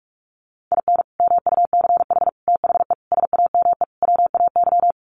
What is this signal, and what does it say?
Morse code, heard as a keyed tone, about 44 wpm, 700 Hz. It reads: IN MUCH THE SAME WAY
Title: Sets of 5 Words Encore 45wpm